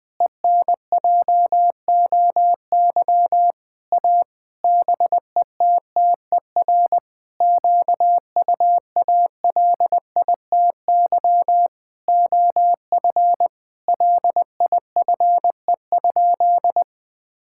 Morse code, heard as a keyed tone, about 20 wpm, 700 Hz. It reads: ENJOY A BETTER QUALITY OF LIFE?
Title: Sentences from Top 1000 Words - Review Part 5 - 20wpm